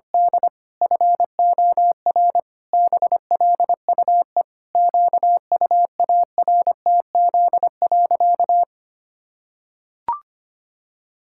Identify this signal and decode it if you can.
Morse code, heard as a keyed tone, about 25 wpm, 700 Hz. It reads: D FOR BLUE QUARTZ.
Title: Pangram Challenge v2 - 25wpm